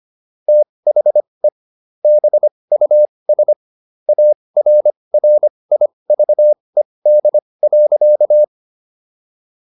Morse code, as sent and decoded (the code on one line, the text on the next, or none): - .... / -... ..- ... / .- .-. .-. .. ...- . -.. .-.-.-
THE BUS ARRIVED.